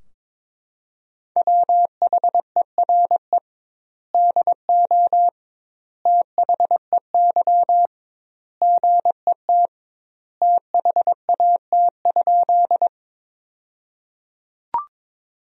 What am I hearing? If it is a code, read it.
Morse code, heard as a keyed tone, about 22 wpm, 700 Hz. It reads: WHERE DO THEY GET THAT?
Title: 2x Word-Spacing - Sentences from Top 200 Words - 22wpm